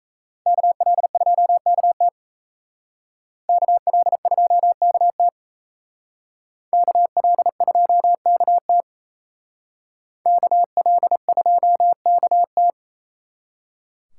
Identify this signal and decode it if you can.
Morse code, heard as a keyed tone, about 36 wpm, 700 Hz. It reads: KL2KT KL2KT KL2KT KL2KT